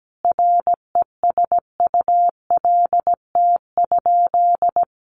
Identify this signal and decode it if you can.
Morse code, heard as a keyed tone, about 17 wpm, 700 Hz. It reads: RESULT?
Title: Sentences from Top 700 Words 17wpm